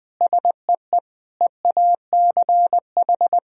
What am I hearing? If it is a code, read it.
Morse code, heard as a keyed tone, about 21 wpm, 700 Hz. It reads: SEE EACH
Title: Sentences from Top 1000 Words - Review Part 1 - 20wpm